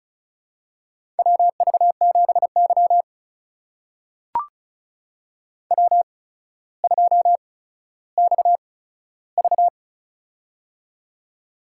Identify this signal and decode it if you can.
Morse code, heard as a keyed tone, about 36 wpm, spaced as for 15 wpm, 700 Hz. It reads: WV7Y W2XV
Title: USA Call Signs - 4 Characters - 8x Character Spacing - 35wpm